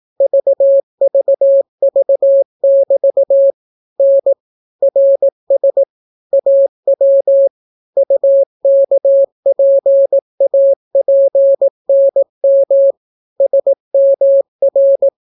Morse code, heard as a keyed tone, about 17 wpm, 550 Hz. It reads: VVV<BT> N RS AW UKPAPNM SMR